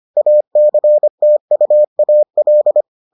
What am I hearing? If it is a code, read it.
Morse code, heard as a keyed tone, about 25 wpm, 600 Hz. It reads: ACTUAL